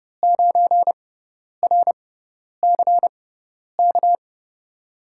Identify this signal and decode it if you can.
Morse code, heard as a keyed tone, about 31 wpm, spaced as for 10 wpm, 700 Hz. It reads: 9RCK
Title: USA Call Signs - 6 Characters - 6x Character Spacing - 30wpm